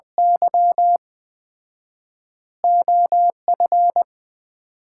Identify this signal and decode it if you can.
Morse code, heard as a keyed tone, about 21 wpm, 700 Hz. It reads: Y OF
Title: 4x Word-Spacing - Sets of 3 Words - 20wpm